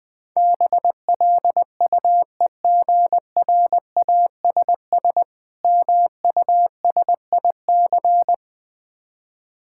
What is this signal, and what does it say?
Morse code, heard as a keyed tone, about 20 wpm, 700 Hz. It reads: BLUEGRASS MUSIC
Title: Difficult Words - Speed-Racing with Hints - 20wpm